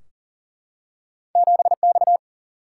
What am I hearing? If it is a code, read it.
Morse code, heard as a keyed tone, about 41 wpm, 700 Hz. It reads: 7X
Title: Single-Letter Single-Number 40wpm